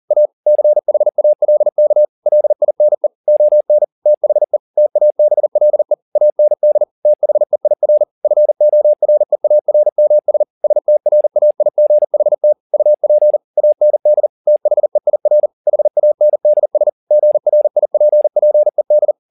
Morse code, heard as a keyed tone, about 40 wpm, 600 Hz. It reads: A CHALK LINE ON THE TABLE AND THEIR FOREARMS STRAIGHT UP AND THEIR HANDS GRIPPED